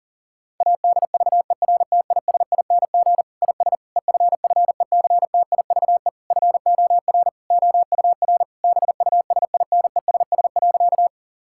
Morse code, heard as a keyed tone, about 37 wpm, 700 Hz. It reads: ADVERTISING IS EFFECTIVE FOR OUR BUSINESS.
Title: Sentences from Top 1000 Words - Review Part 6 - 40wpm